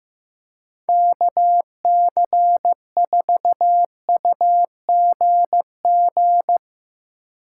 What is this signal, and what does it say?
Morse code, heard as a keyed tone, about 15 wpm, 700 Hz. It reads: KC4UGG
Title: USA Call Signs - 6 Characters - 8x Character Spacing - 15wpm